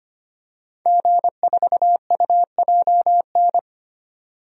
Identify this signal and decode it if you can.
Morse code, heard as a keyed tone, about 25 wpm, 700 Hz. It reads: G4UJN
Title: DXCC-100 Call Signs - 5 Characters - 25wpm